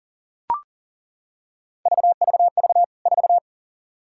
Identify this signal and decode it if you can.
Morse code, heard as a keyed tone, about 40 wpm, 700 Hz. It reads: VVV 4
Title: K1USN SST QSO Practice 40wpm